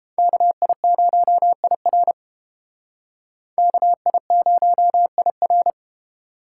Morse code, heard as a keyed tone, about 30 wpm, 700 Hz. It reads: KI0IR KI0IR